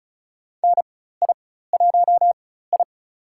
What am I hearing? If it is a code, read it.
Morse code, heard as a keyed tone, about 36 wpm, spaced as for 16 wpm, 700 Hz. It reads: NI1I
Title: USA Call Signs - 4 Characters - 4x Character Spacing - 35wpm